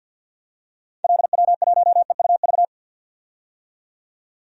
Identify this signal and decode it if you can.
Morse code, heard as a keyed tone, about 51 wpm, 700 Hz. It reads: LW1EUV
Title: DXCC-100 Call Signs - 6 Characters - 50wpm